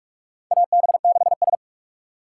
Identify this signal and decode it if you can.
Morse code, heard as a keyed tone, about 46 wpm, 700 Hz. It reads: AB6S
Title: USA Call Signs - 4 Characters - 4x Character Spacing - 45wpm